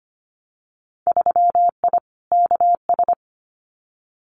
Morse code, heard as a keyed tone, about 25 wpm, 700 Hz. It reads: 3I KS
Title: Winter Field Day Exchanges Encore 25wpm